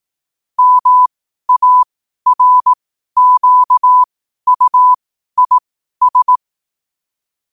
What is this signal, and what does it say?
Morse code, heard as a keyed tone, about 18 wpm, spaced as for 12 wpm, 1000 Hz. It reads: MARQUIS
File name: qotdfiles / 12wpm